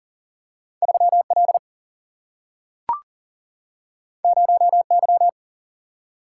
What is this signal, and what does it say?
Morse code, heard as a keyed tone, about 40 wpm, 700 Hz. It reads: 3 L 0 Y
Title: Single-Letter Single-Number 40wpm